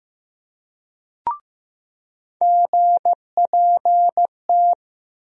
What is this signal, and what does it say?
Morse code, heard as a keyed tone, about 16 wpm, 700 Hz. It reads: GPT
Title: USA Call Sign Postfixes - 3 Letters 15wpm